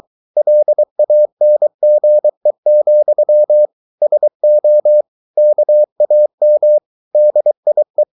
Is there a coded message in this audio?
Yes: Morse code from 0.4 to 8.0 s, heard as a keyed tone, about 23 wpm, 600 Hz. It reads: LANGE, SO KAM DIE